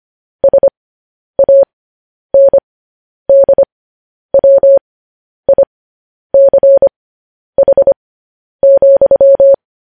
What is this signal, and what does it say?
Morse code, heard as a keyed tone, about 25 wpm, spaced as for 10 wpm, 550 Hz. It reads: SANDWICH,